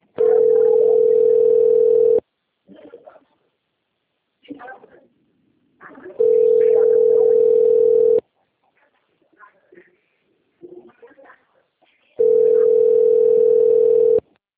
Ringback tone (North America), 440 + 480 Hz.
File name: rg-1600-unknown- - - .10383